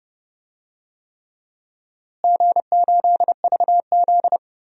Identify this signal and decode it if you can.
Morse code, heard as a keyed tone, about 30 wpm, 700 Hz. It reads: G8VZ
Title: DXCC-100 Call Signs - 4 Characters - Repeated 3x - 30wpm